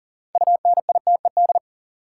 Morse code, heard as a keyed tone, about 40 wpm, 700 Hz. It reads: UNITED
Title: Sentences from Top 600 Words - New Word Drill - 40wpm